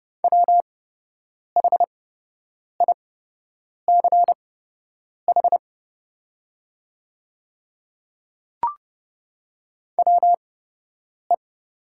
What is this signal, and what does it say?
Morse code, heard as a keyed tone, about 30 wpm, spaced as for 8 wpm, 700 Hz. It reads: WHICH WE
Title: 8x Character Spacing - Top 100 Words - 30wpm